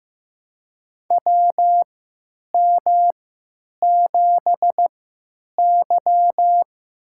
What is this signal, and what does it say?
Morse code, heard as a keyed tone, about 15 wpm, spaced as for 9 wpm, 700 Hz. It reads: WM7Y
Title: USA Call Signs - 4 Characters - 3x Character Spacing - 15wpm